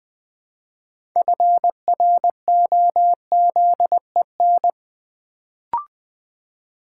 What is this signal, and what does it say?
Morse code, heard as a keyed tone, about 20 wpm, 700 Hz. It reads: FROZEN
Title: Build A Word - 3 Letters At A Time - 6-Letter Words - 20wpm